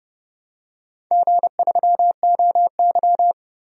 Morse code, heard as a keyed tone, about 29 wpm, 700 Hz. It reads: G3OY
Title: DXCC-100 Call Signs - 4 Characters - 30wpm